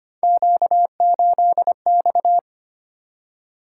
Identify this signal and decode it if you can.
Morse code, heard as a keyed tone, about 25 wpm, 700 Hz. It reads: Q8X
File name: Three Letter-Number Combinations - 25wpm